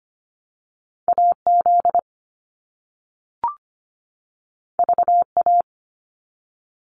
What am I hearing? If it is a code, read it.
Morse code, heard as a keyed tone, about 25 wpm, 700 Hz. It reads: AZ VA
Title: US States 25wpm